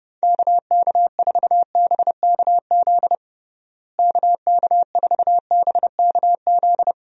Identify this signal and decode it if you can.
Morse code, heard as a keyed tone, about 30 wpm, 700 Hz. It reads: KK4BKZ KK4BKZ
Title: USA Call Signs - 6 Characters - Repeated 3x - 30wpm